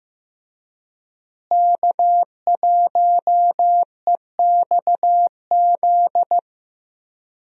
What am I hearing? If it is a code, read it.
Morse code, heard as a keyed tone, about 15 wpm, 700 Hz. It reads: K1EXZ
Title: USA Call Signs - 5 Characters - Repeated 3x - 15wpm